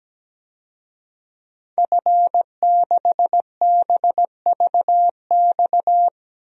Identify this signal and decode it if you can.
Morse code, heard as a keyed tone, about 17 wpm, 700 Hz. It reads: F6BVX
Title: DXCC-100 Call Signs - 5 Characters - 17wpm